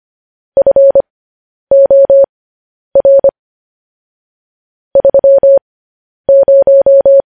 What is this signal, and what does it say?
Morse code, heard as a keyed tone, about 25 wpm, spaced as for 10 wpm, 550 Hz. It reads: FOR 30
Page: news-headlines-in-morse-code-at-10-wpm